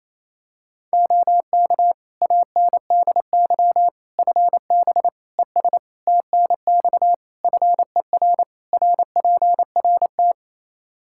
Morse code, heard as a keyed tone, about 28 wpm, 700 Hz. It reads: OK ANDY FB ES TNX FER RPRT
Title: K1USN SST QSO Practice 28wpm